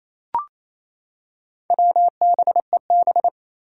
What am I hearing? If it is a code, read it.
Morse code, heard as a keyed tone, about 28 wpm, 700 Hz. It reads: WBEB
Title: Four Challenging Letter-Number Combinations 28wpm